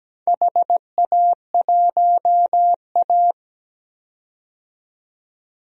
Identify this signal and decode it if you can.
Morse code, heard as a keyed tone, about 17 wpm, 700 Hz. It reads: HA1A
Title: DXCC-100 Call Signs - 4 Characters - 17wpm